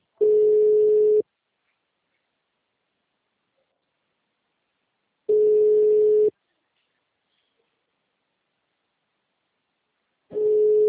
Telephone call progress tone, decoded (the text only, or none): ringback tone